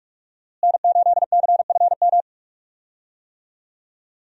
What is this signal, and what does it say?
Morse code, heard as a keyed tone, about 47 wpm, 700 Hz. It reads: N8CFM
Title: USA Call Signs - 5 Characters - Repeated 3x - 45wpm